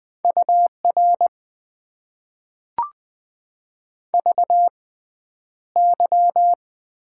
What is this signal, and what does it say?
Morse code, heard as a keyed tone, about 20 wpm, spaced as for 10 wpm, 700 Hz. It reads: UR VY
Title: Top 90 QSO Elements - 6x Character Spacing - 20wpm